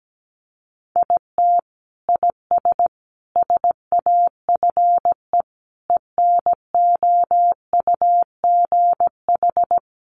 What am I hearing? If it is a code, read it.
Morse code, heard as a keyed tone, about 17 wpm, 700 Hz. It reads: IT IS SAFE ENOUGH
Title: Sentences from Top 900 Words 17wpm